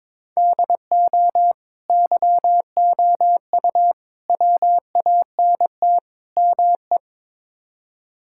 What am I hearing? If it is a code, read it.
Morse code, heard as a keyed tone, about 23 wpm, 700 Hz. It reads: DO YOU WANT ME
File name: practice / 22wpm / Build A Phrase - 5-Word Phrases - 22wpm